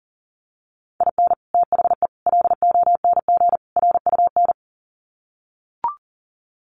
Morse code, heard as a keyed tone, about 40 wpm, 700 Hz. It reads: IN THE LONG RUN
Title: Sets of 4 Words 40wpm